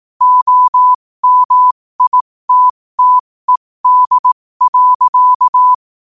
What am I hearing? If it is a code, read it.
Morse code, heard as a keyed tone, about 18 wpm, spaced as for 15 wpm, 1000 Hz. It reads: OMITTED.
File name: qotdfiles / 15wpm